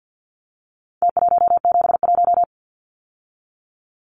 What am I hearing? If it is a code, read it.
Morse code, heard as a keyed tone, about 50 wpm, 700 Hz. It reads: T171
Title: Numbers - Contest Style - 4-Digit 50wpm